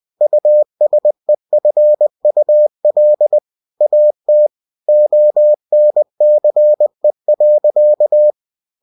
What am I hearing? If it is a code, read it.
Morse code, heard as a keyed tone, about 20 wpm, 600 Hz. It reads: USEFUL AT ONCE.
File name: cw-books / winnie-the-pooh / winnie-20wpm-page-0063